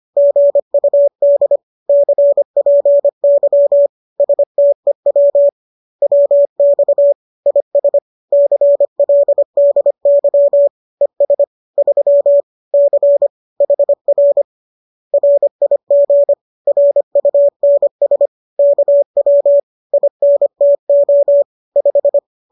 Morse code, heard as a keyed tone, about 25 wpm, 570 Hz. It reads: GUD CPY STEW WX IS CLDY ES 3 C HR RIG RUNS KW INTO 5